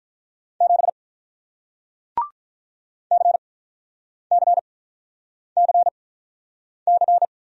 Morse code, heard as a keyed tone, about 48 wpm, spaced as for 9 wpm, 700 Hz. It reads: 7 CCCC